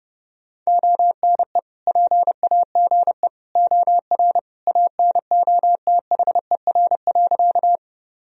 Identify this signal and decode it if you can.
Morse code, heard as a keyed tone, about 32 wpm, 700 Hz. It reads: ONE PAGE OR ANOTHER.
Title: Sentences from Top 1000 Words - Review Part 1 - 30wpm